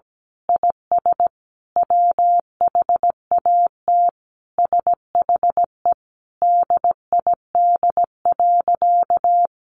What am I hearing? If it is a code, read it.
Morse code, heard as a keyed tone, about 17 wpm, 700 Hz. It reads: IS WHAT SHE DID.
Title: Sentences from Top 100 Words 17wpm